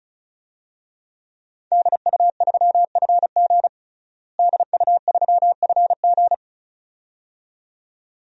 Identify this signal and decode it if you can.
Morse code, heard as a keyed tone, about 36 wpm, 700 Hz. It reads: DU3FG DU3FG
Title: DXCC-100 Call Signs - 5 Characters - Repeated 2x - 35wpm